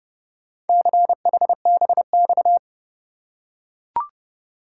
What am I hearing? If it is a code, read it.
Morse code, heard as a keyed tone, about 27 wpm, 700 Hz. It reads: CHBX